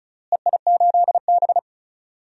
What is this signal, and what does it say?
Morse code, heard as a keyed tone, about 34 wpm, 700 Hz. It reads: EI8B
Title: DXCC-100 Call Signs - 4 Characters - Repeated 3x - 35wpm